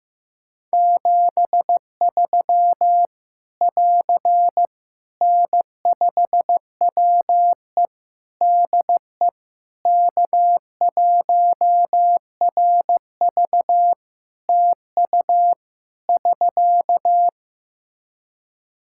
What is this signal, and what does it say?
Morse code, heard as a keyed tone, about 16 wpm, 700 Hz. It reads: 73 <AR> N5WE DE K1RV TU <SK>
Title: Practice QSO - Exchange 4 - Ending - 15wpm